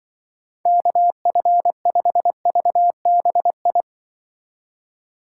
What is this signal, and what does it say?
Morse code, heard as a keyed tone, about 24 wpm, 700 Hz. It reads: KF5VBI
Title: Speed Racing - USA 2x3 Call Signs 20wpm